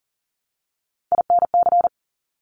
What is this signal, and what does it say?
Morse code, heard as a keyed tone, about 38 wpm, 700 Hz. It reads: INC
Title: Top Three Letter Combinations in Words 40wpm